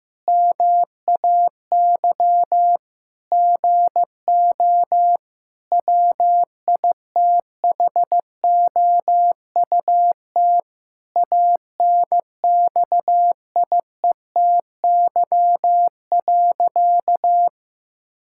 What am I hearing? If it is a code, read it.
Morse code, heard as a keyed tone, about 15 wpm, 700 Hz. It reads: MAY GO WITHOUT ANXIETY.
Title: Sentences from Top 1600 Words 15wpm